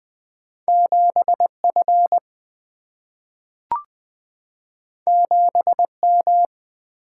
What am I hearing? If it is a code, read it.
Morse code, heard as a keyed tone, about 20 wpm, 700 Hz. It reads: 7 F 7 M
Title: Single-Letter Single-Number 20wpm